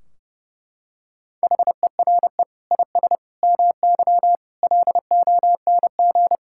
Morse code, heard as a keyed tone, about 30 wpm, 700 Hz. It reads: HERE IS MY LONG
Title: Sentences from Top 1600 Words 30wpm